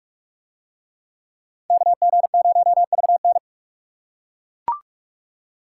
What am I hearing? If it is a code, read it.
Morse code, heard as a keyed tone, about 39 wpm, 700 Hz. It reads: KG0VN